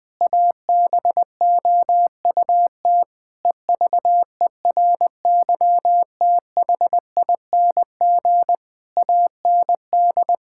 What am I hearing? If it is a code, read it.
Morse code, heard as a keyed tone, about 20 wpm, 700 Hz. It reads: ABOUT EVERYTHING AND
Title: Sentences from Top 1600 Words 20wpm